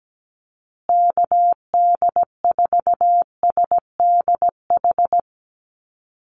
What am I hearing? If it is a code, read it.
Morse code, heard as a keyed tone, about 17 wpm, 700 Hz. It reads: KD4SDH
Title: USA Call Signs Repeated 2x 17wpm